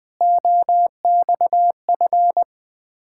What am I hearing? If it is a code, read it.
Morse code, heard as a keyed tone, about 20 wpm, 690 Hz. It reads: OXF
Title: Ontario QSO Party Exchange 20wpm